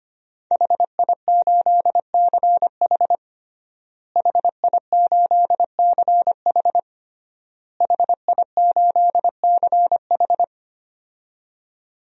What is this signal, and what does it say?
Morse code, heard as a keyed tone, about 25 wpm, 700 Hz. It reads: HI8CH HI8CH HI8CH
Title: DXCC-100 Call Signs - 5 Characters - Repeated 3x - 25wpm